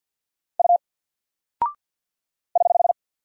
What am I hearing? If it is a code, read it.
Morse code, heard as a keyed tone, about 50 wpm, 700 Hz. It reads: U <HH>
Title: Single Character (Letter:Number:Punctuation:Prosign) Alt 50wpm